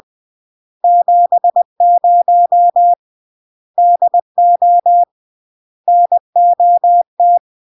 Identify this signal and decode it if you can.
Morse code, heard as a keyed tone, about 20 wpm, 700 Hz. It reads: 70 DO NOT
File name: CW / jodorowsky / 82 commandments 20-20